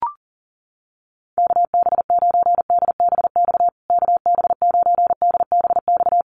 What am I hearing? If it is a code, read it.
Morse code, heard as a keyed tone, about 37 wpm, 700 Hz. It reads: KB9DBX KB9DBX
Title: USA Call Signs Repeated 3x 40wpm